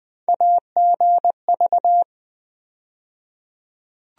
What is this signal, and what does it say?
Morse code, heard as a keyed tone, about 20 wpm, 700 Hz. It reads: AGV
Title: USA Call Sign Postfixes - 3 Letters 20wpm